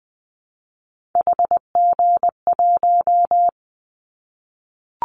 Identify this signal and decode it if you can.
Morse code, heard as a keyed tone, about 20 wpm, 700 Hz. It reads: HG1
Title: Callsign Prefixes for Top 25 DXCC - Repeated 3x - 20wpm